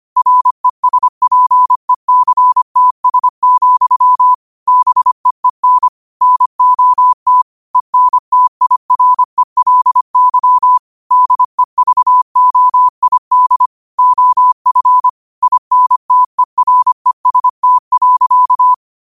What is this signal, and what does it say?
Morse code, heard as a keyed tone, about 25 wpm, 1000 Hz. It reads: RESPECTS, BEEN NOT ENTIRELY DEVOID OF INTEREST.
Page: Quotes for Wed, 13 Aug 2025 in Morse Code at 25 words per minute.